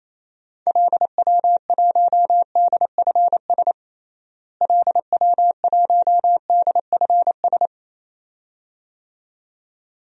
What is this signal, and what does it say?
Morse code, heard as a keyed tone, about 27 wpm, 700 Hz. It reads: LW1DFS LW1DFS
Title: DXCC-100 Call Signs - 6 Characters - Repeated 2x - 28wpm